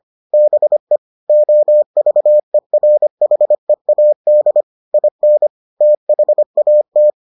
Morse code, heard as a keyed tone, about 25 wpm, 600 Hz. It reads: BE OVERHEAD IN THAT